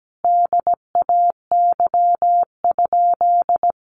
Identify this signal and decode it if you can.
Morse code, heard as a keyed tone, about 17 wpm, 700 Hz. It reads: DAY?
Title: Sentences from Top 1100 Words 17wpm